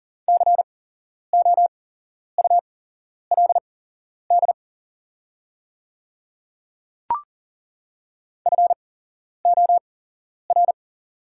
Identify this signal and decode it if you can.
Morse code, heard as a keyed tone, about 40 wpm, spaced as for 11 wpm, 700 Hz. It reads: COULD FOR
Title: 8x Character Spacing - Top 100 Words - 40wpm